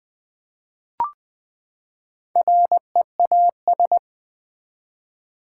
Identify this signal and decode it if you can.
Morse code, heard as a keyed tone, about 20 wpm, 700 Hz. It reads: REAS
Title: Build A Word - 4 Letters At A Time - 8-Letter Words - 20wpm